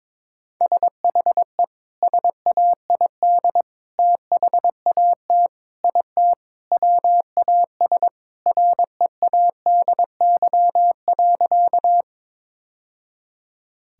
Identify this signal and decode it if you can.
Morse code, heard as a keyed tone, about 22 wpm, 700 Hz. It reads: SHE SAID THAT IT WAS READY.
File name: practice / 22wpm / Sentences with 3-Word Phrases - 22wpm